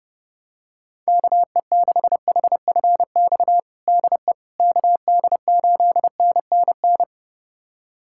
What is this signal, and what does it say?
Morse code, heard as a keyed tone, about 31 wpm, 700 Hz. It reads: KE6HFX DE KD8NNN
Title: USA Callsign DE Callsign - 6 Characters - 30wpm